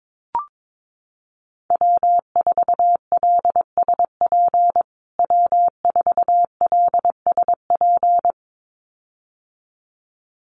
Morse code, heard as a keyed tone, about 22 wpm, 700 Hz. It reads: W4LSP W4LSP
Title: USA Call Signs Repeated 2x 22wpm